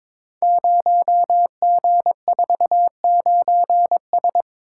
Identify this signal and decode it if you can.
Morse code, heard as a keyed tone, about 22 wpm, 700 Hz. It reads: 0G49S
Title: Code Groups - 5 Letter:Numbers - 22wpm